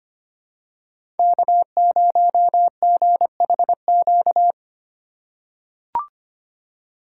Morse code, heard as a keyed tone, about 25 wpm, 700 Hz. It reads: K0GHQ
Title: USA Call Signs - 5 Characters - 4x Character Spacing - 25wpm